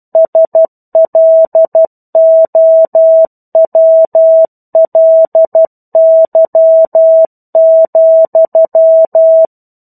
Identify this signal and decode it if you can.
Morse code, heard as a keyed tone, about 12 wpm, 650 Hz. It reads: SLOWLY,
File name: 01-Chapter-01 12 WPM